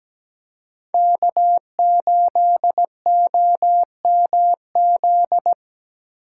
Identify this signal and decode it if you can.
Morse code, heard as a keyed tone, about 20 wpm, 700 Hz. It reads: K8OMZ
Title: Speed Racing - USA 1x3 Call Signs 17wpm